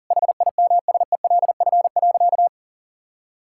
Morse code, heard as a keyed tone, about 37 wpm, 700 Hz. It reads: HIMSELF.
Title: Sentences from Top 1600 Words 40wpm